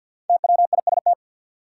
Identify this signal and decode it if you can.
Morse code, heard as a keyed tone, about 52 wpm, 700 Hz. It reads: TWIST
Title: Five Letter Words - 50wpm